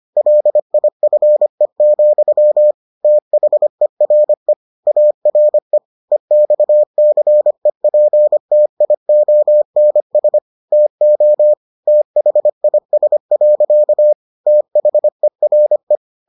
Morse code, heard as a keyed tone, about 25 wpm, 600 Hz. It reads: LIFE, THERE ARE EXCEPTIONS TO THIS. THERE